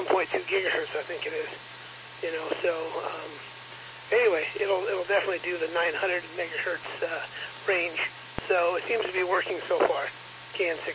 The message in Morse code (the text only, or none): none